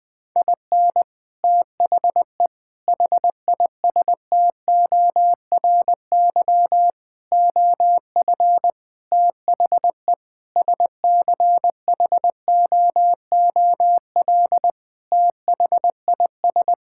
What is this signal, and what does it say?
Morse code, heard as a keyed tone, about 19 wpm, 700 Hz. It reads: IN THE HISTORY OF THE SCHOOL THIS
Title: Sentences with 5-Word Phrases - Spoken Beginning - 20wpm